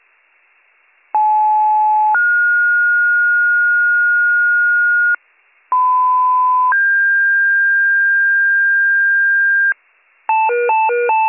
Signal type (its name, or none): two-tone page